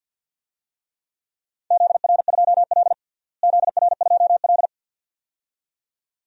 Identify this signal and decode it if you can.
Morse code, heard as a keyed tone, about 53 wpm, 700 Hz. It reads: ZR2L ZR2L
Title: DXCC-100 Call Signs - 4 Characters - Repeated 2x - 50wpm